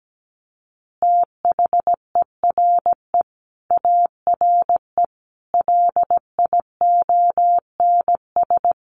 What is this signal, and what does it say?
Morse code, heard as a keyed tone, about 17 wpm, 700 Hz. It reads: THERE ARE LIONS
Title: Easy Phrases 17wpm